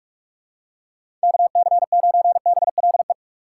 Morse code, heard as a keyed tone, about 46 wpm, 700 Hz. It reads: KC9BLE
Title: USA Call Signs - 6 Characters - 4x Character Spacing - 45wpm